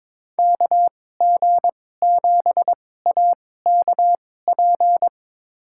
Morse code, heard as a keyed tone, about 22 wpm, spaced as for 16 wpm, 700 Hz. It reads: KG7AKP
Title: USA Call Signs - 6 Characters - 2x Character Spacing - 22wpm